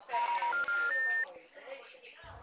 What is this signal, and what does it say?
Special information tone, with speech.